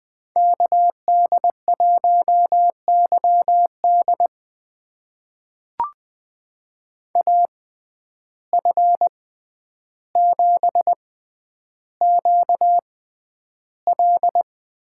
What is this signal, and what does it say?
Morse code, heard as a keyed tone, about 20 wpm, spaced as for 10 wpm, 700 Hz. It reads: KD1YD AF7QL
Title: USA Call Signs - 5 Characters - 6x Character Spacing - 20wpm